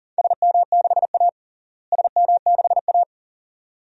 Morse code, heard as a keyed tone, about 40 wpm, 700 Hz. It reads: SM6A SM6A
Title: DXCC-100 Call Signs - 4 Characters - Repeated 2x - 40wpm